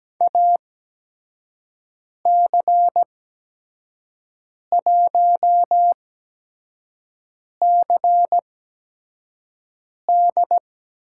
Morse code, heard as a keyed tone, about 17 wpm, spaced as for 5 wpm, 700 Hz. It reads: AC1CD
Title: USA Call Signs - 5 Characters - 8x Character Spacing - 17wpm